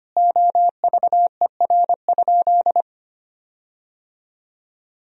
Morse code, heard as a keyed tone, about 25 wpm, 700 Hz. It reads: OVER?
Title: 4x Word-Spacing - Sentences from Top 100 Words - 25wpm